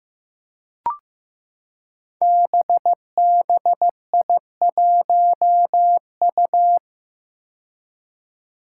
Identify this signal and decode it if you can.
Morse code, heard as a keyed tone, about 15 wpm, 700 Hz. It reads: BBI1U
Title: Five Challenging Letter-Number Combinations 15wpm